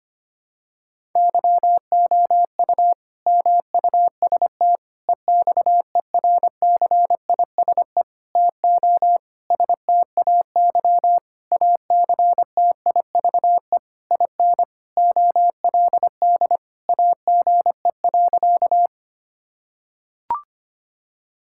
Morse code, heard as a keyed tone, about 25 wpm, 700 Hz. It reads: YOU MUST EXERCISE TO STAY ACTIVE IN OLD AGE.
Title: Sentences from Top 1000 Words - Review Part 6 - 25wpm